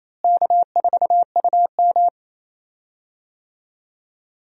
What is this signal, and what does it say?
Morse code, heard as a keyed tone, about 28 wpm, 700 Hz. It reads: K4UM
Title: USA Call Signs - 4 Characters - Repeated 2x - 28wpm